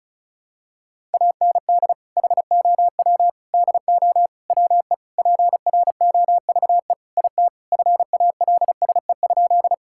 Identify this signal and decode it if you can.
Morse code, heard as a keyed tone, about 34 wpm, 700 Hz. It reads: AND HOW DO WE PROVE IT FALSE?
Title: Sentences from Top 1600 Words 35wpm